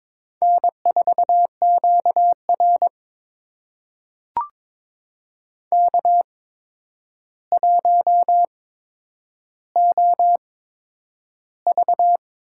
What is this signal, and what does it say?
Morse code, heard as a keyed tone, about 22 wpm, spaced as for 9 wpm, 700 Hz. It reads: N4QR K1OV
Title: USA Call Signs - 4 Characters - 8x Character Spacing - 22wpm